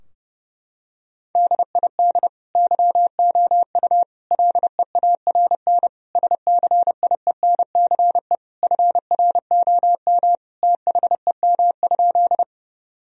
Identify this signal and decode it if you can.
Morse code, heard as a keyed tone, about 32 wpm, 700 Hz. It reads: DID YOU LEARN SCIENCE FROM THEM?
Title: Sentences from Top 1000 Words - Review Part 2 - 30wpm